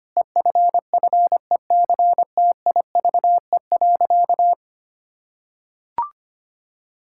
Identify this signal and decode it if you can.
Morse code, heard as a keyed tone, about 25 wpm, 700 Hz. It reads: EFFECTIVE.
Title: Sentences from Top 1000 Words - Review Part 6 - 25wpm